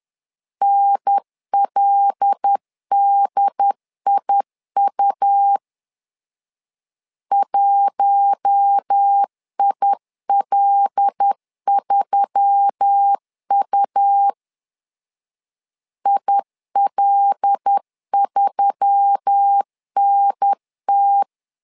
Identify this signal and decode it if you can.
Morse code, heard as a keyed tone, about 10 wpm, 800 Hz. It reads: NLDIU 1IL3U IL3NT
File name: LEZIONI / 5 LBVC / 5 L 10wpm 2259